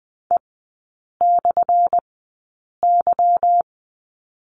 Morse code, heard as a keyed tone, about 20 wpm, spaced as for 8 wpm, 700 Hz. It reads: E/Y
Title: 178 Lesson 29e - Slash - Three Characters 20wpm